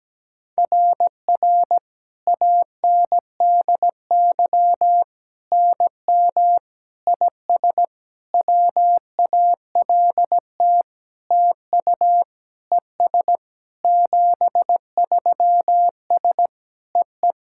Morse code, heard as a keyed tone, about 17 wpm, 700 Hz. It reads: RR ANDY NM IS WALT TU ES 73S EE
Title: SOTA QSO Practice 17wpm